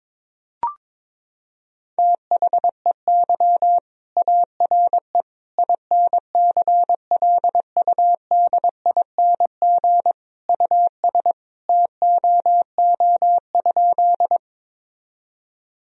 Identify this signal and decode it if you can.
Morse code, heard as a keyed tone, about 22 wpm, 700 Hz. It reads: THEY ARE INCLUDING US TOO?
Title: Sentences from Top 1000 Words - Review Part 4 - 22wpm